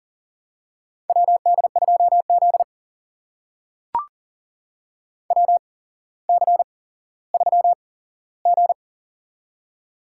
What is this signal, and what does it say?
Morse code, heard as a keyed tone, about 40 wpm, spaced as for 17 wpm, 700 Hz. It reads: WD2Z WC3G
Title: USA Call Signs - 4 Characters - 8x Character Spacing - 40wpm